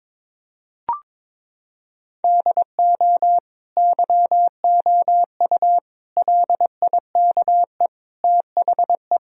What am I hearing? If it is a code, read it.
Morse code, heard as a keyed tone, about 22 wpm, 700 Hz. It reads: DO YOU LIKE THE
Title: Sentences from Top 1000 Words - Review Part 4 - 22wpm